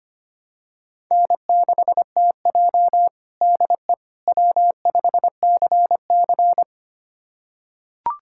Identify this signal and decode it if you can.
Morse code, heard as a keyed tone, about 25 wpm, 700 Hz. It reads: N6TJ DE W5CC
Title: USA Callsign DE Callsign - 4 Characters - 25wpm